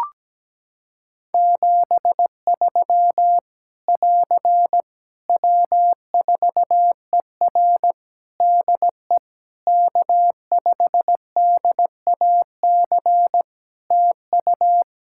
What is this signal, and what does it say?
Morse code, heard as a keyed tone, about 17 wpm, 700 Hz. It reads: 73 <AR> W4ER DE K5DAC TU
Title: K1USN SST QSO Practice 17wpm